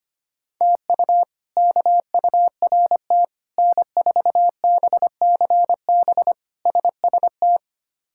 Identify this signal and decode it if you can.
Morse code, heard as a keyed tone, about 25 wpm, 700 Hz. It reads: TU KURT N4BCB SST
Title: K1USN SST QSO Practice - 25wpm